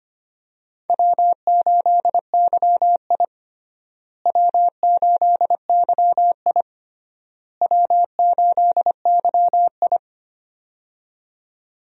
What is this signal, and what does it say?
Morse code, heard as a keyed tone, about 25 wpm, 700 Hz. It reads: W8YI W8YI W8YI
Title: USA Call Signs - 4 Characters - Repeated 3x - 25wpm